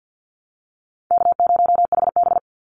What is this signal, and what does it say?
Morse code, heard as a keyed tone, about 50 wpm, 700 Hz. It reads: K0HB
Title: Call Signs Repeated 2x - 50wpm